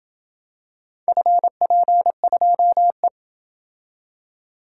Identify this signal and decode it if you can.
Morse code, heard as a keyed tone, about 27 wpm, 700 Hz. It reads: FP2E